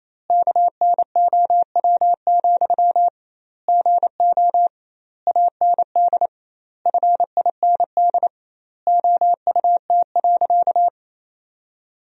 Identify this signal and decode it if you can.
Morse code, heard as a keyed tone, about 27 wpm, 700 Hz. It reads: KNOW, GO AND FIND OUT.
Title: 2x Word-Spacing - Sentences from Top 200 Words - 28wpm